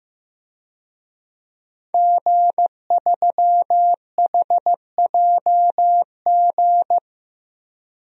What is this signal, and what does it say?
Morse code, heard as a keyed tone, about 15 wpm, 700 Hz. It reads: G3HJG
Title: DXCC-100 Call Signs - 5 Characters - 15wpm